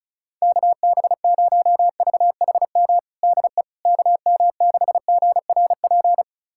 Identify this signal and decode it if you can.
Morse code, heard as a keyed tone, about 36 wpm, 700 Hz. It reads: KB0VHM DE KM6GRP
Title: USA Callsign DE Callsign - 6 Characters - 35wpm